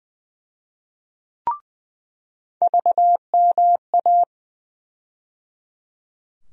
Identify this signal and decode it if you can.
Morse code, heard as a keyed tone, about 19 wpm, 700 Hz. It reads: VMA
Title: USA Call Sign Postfixes - 3 Letters 20wpm